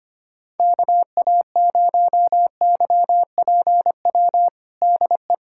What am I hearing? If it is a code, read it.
Morse code, heard as a keyed tone, about 25 wpm, 700 Hz. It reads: KA0YPW DE
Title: USA Callsign DE Callsign - 6 Characters - 25wpm